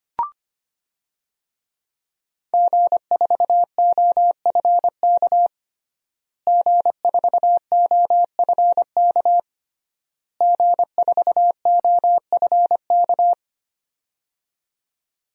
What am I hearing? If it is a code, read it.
Morse code, heard as a keyed tone, about 25 wpm, 700 Hz. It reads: G4OFK G4OFK G4OFK
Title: DXCC-100 Call Signs - 5 Characters - Repeated 3x - 25wpm